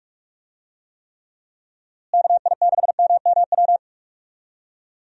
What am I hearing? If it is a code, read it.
Morse code, heard as a keyed tone, about 46 wpm, 700 Hz. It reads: KI6MMW
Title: USA Call Signs - 6 Characters - Repeated 2x - 45wpm